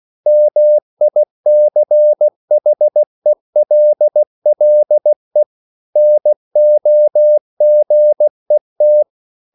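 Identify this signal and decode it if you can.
Morse code, heard as a keyed tone, about 16 wpm, 600 Hz. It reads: MICHELLE NOGET
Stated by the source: Slow (QRS) Morse code at 16 wpm, from CW med Gnister.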